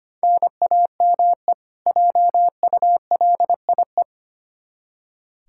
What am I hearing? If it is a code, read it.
Morse code, heard as a keyed tone, about 25 wpm, 700 Hz. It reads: NAME JULIE
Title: K1USN SST QSO Practice 25wpm